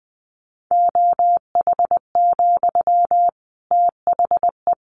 Morse code, heard as a keyed tone, about 20 wpm, 700 Hz. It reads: OH, THE
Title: A Sound Of Thunder 20wpm